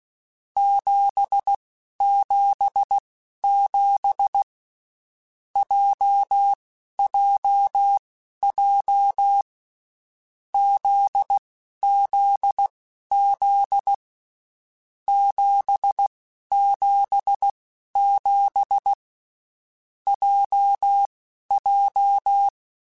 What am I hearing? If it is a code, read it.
Morse code, heard as a keyed tone, about 16 wpm, spaced as for 11 wpm, 800 Hz. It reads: 777 JJJ ZZZ 777 JJ
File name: Basic Sets / Basic Letter Groups Collective7 / JZ17@10